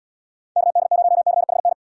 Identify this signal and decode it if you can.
Morse code, heard as a keyed tone, about 77 wpm, 700 Hz. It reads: KD0CHN